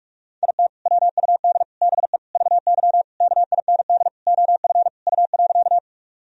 Morse code, heard as a keyed tone, about 46 wpm, 700 Hz. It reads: IT WUD BE VY KIND OF U.
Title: Sentences from Top 200 Words and Abbreviations 45wpm